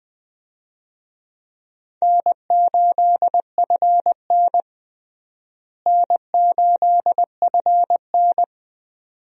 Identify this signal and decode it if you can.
Morse code, heard as a keyed tone, about 20 wpm, 700 Hz. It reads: N8FN N8FN
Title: USA Call Signs - 4 Characters - Repeated 2x - 20wpm